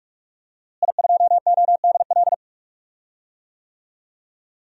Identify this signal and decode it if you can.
Morse code, heard as a keyed tone, about 41 wpm, 700 Hz. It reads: I2ODL
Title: DXCC-100 Call Signs - 5 Characters - Repeated 3x - 45wpm